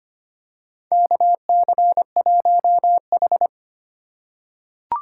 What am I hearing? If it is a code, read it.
Morse code, heard as a keyed tone, about 25 wpm, 700 Hz. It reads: KC1H
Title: USA Call Signs - 4 Characters - 4x Character Spacing - 25wpm